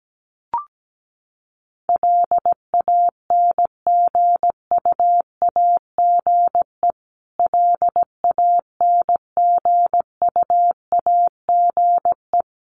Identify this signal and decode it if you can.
Morse code, heard as a keyed tone, about 17 wpm, 700 Hz. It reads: LANGUAGE LANGUAGE
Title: Top 500 Words Repeated 17wpm